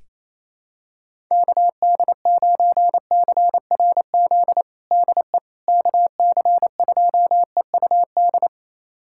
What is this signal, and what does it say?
Morse code, heard as a keyed tone, about 28 wpm, 700 Hz. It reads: KD9CRZ DE KC2EUD
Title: USA Callsign DE Callsign 28wpm